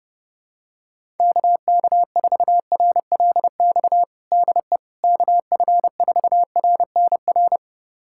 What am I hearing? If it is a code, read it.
Morse code, heard as a keyed tone, about 30 wpm, 700 Hz. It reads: KK4RLX DE KF4RNR
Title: USA Callsign DE Callsign - 6 Characters - 30wpm